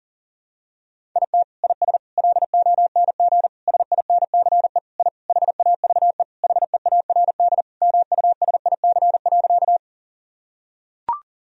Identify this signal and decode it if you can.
Morse code, heard as a keyed tone, about 37 wpm, 700 Hz. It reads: IT IS LONG SINCE I HAVE HEARD MUSIC.
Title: Sentences from Top 1000 Words - Review Part 1 - 40wpm